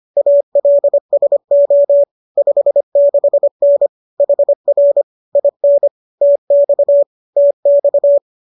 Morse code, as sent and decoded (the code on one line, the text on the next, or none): .- .-.. ... --- / ..... -.... -. / .... .-. / .. -. / - -..- / - -..-
ALSO 56N HR IN TX TX